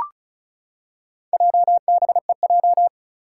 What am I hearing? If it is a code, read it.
Morse code, heard as a keyed tone, about 34 wpm, 700 Hz. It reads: JBEJ
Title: Four Challenging Letter-Number Combinations 35wpm